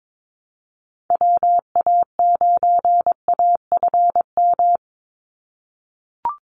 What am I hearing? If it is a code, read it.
Morse code, heard as a keyed tone, about 22 wpm, 700 Hz. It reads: WA9AFM
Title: USA Call Signs 22wpm